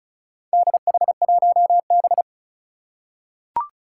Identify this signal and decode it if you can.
Morse code, heard as a keyed tone, about 36 wpm, 700 Hz. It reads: DH1B
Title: Four Challenging Letter-Number Combinations 35wpm